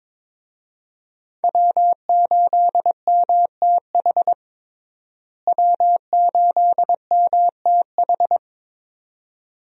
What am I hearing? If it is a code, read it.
Morse code, heard as a keyed tone, about 21 wpm, 700 Hz. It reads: W8MTH W8MTH
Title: USA Call Signs - 5 Characters - Repeated 2x - 22wpm